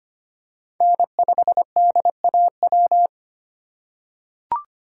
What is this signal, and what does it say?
Morse code, heard as a keyed tone, about 25 wpm, 700 Hz. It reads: N5DAW
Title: USA Call Signs - 5 Characters - 4x Character Spacing - 25wpm